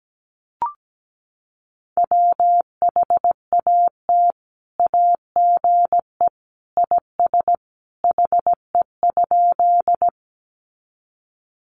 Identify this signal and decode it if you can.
Morse code, heard as a keyed tone, about 17 wpm, 700 Hz. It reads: WHAT AGE IS HE?
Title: Sentences from Top 500 Words 17wpm